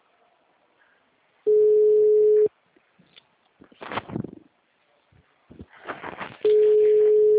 Ringback tone (Europe, 425 Hz).